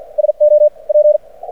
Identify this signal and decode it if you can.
Morse code, heard as a keyed tone, about 53 wpm, spaced as for 44 wpm, 600 Hz. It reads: SO P
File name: archived / CRSnet